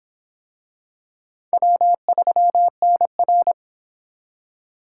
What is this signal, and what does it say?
Morse code, heard as a keyed tone, about 24 wpm, 700 Hz. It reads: W3NR